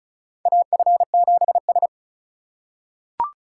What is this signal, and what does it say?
Morse code, heard as a keyed tone, about 34 wpm, 700 Hz. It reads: AF7S
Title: USA Call Signs - 4 Characters - 2x Character Spacing - 35wpm